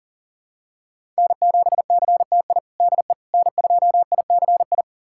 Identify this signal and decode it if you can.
Morse code, heard as a keyed tone, about 40 wpm, 700 Hz. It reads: N7CTI DE N2ICI
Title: USA Callsign DE Callsign - 5 Characters - 40wpm